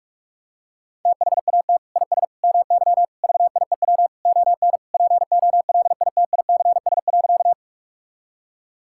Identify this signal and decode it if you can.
Morse code, heard as a keyed tone, about 46 wpm, 700 Hz. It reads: THAT IS MY VIEW ON POLITICS.
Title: Sentences from Top 1000 Words - Review Part 6 - 45wpm